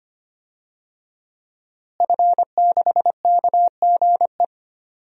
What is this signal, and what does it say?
Morse code, heard as a keyed tone, about 25 wpm, 700 Hz. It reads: F6KGE
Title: DXCC-100 Call Signs - 5 Characters - Repeated 2x - 25wpm